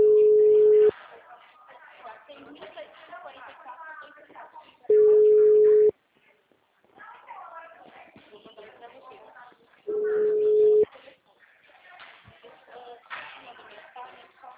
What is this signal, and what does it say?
Ringback tone (Europe, 425 Hz), with speech.
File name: out- -2005- - - .18